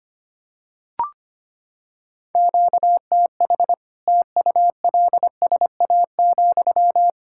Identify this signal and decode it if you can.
Morse code, heard as a keyed tone, about 25 wpm, 700 Hz. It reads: QTH TULSA,
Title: Practice QSO - Exchange 1 - Beginning - 25wpm